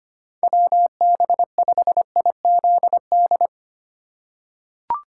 Morse code, heard as a keyed tone, about 25 wpm, 700 Hz. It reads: WB5IZD
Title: USA Call Signs - 2x Character Spacing - 25wpm